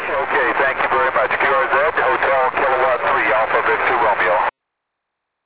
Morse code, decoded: none